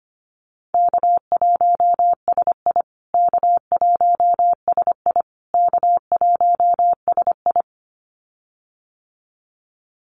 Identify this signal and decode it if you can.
Morse code, heard as a keyed tone, about 25 wpm, 700 Hz. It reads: K1SI K1SI K1SI
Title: USA Call Signs Repeated 3x 25wpm